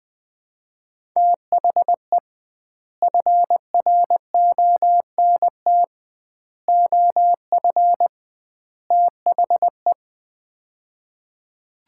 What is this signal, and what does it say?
Morse code, heard as a keyed tone, about 20 wpm, 700 Hz. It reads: THE FRONT OF THE
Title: 2x Word-Spacing - Sets of 4 Words - 20wpm